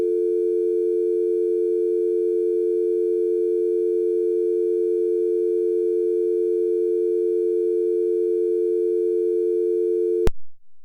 Dial tone (North America), 350 + 440 Hz.
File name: dialToneTest